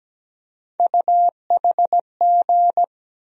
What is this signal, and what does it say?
Morse code, heard as a keyed tone, about 17 wpm, 700 Hz. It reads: UHG
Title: USA Call Sign Postfixes - 3 Letters 17wpm